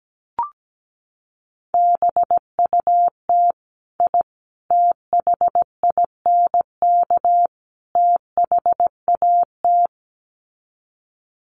Morse code, heard as a keyed tone, about 17 wpm, 700 Hz. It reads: BUT I THINK THAT
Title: Sets of 4 Words 17wpm